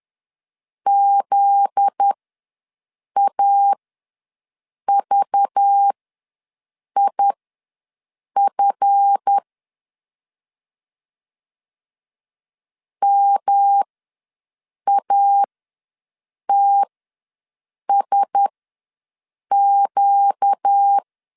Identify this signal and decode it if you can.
Morse code, heard as a keyed tone, about 10 wpm, spaced as for 6 wpm, 800 Hz. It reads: ZAVIF MATSQ
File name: LEZIONI / 7 GF9Z / 7 GF9Z ws 10wpm 3919